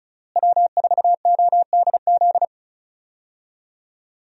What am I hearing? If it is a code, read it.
Morse code, heard as a keyed tone, about 36 wpm, 700 Hz. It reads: W4ODZ
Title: USA Call Signs - 5 Characters - Repeated 3x - 35wpm